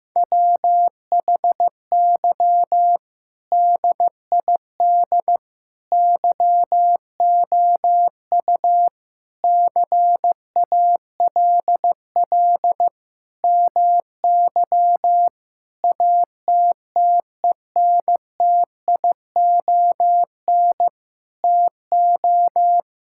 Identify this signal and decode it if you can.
Morse code, heard as a keyed tone, about 15 wpm, 700 Hz. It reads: WHY DID YOU CALL MY ATTENTION TO
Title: Sentences from Top 1000 Words - Review Part 6 - 15wpm